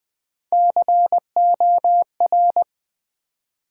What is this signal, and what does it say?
Morse code, heard as a keyed tone, about 20 wpm, 700 Hz. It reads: COR
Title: Build A Word - 3 Letters At A Time - 9-Letter Words - 20wpm